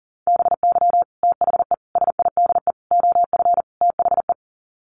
Morse code, heard as a keyed tone, about 40 wpm, 700 Hz. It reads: BY THE SIDE OF THE
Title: Sets of 5 Words Encore2 40wpm